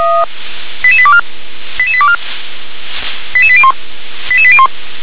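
Five-tone selective call (ZVEI), heard as quick runs of five tones, tones of about 70 ms. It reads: 88824 88824 88801 88801
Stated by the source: Decoded by SDR as ZVEI five-tone selective call.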